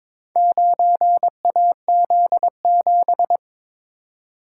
Morse code, heard as a keyed tone, about 22 wpm, 700 Hz. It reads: 9AZ7
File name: practice / 22wpm / Four Letter-Number Combinations - 22wpm